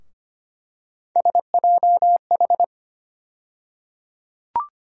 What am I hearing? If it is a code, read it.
Morse code, heard as a keyed tone, about 25 wpm, 700 Hz. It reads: SJH
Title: USA Call Sign Postfixes - 3 Letters 25wpm